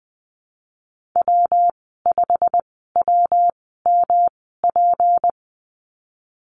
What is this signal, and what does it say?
Morse code, heard as a keyed tone, about 19 wpm, spaced as for 14 wpm, 700 Hz. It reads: W5WMP
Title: 186 Lesson 30g - Letter b - Call Signs 20wpm